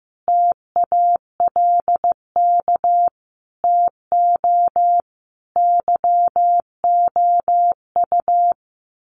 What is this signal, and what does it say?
Morse code, heard as a keyed tone, about 15 wpm, 700 Hz. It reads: TALK TO YOU
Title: Sets of 5 Words Encore 15wpm